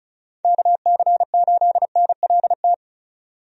Morse code, heard as a keyed tone, about 34 wpm, 700 Hz. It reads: KC8NLT
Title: USA Call Signs - 6 Characters - 35wpm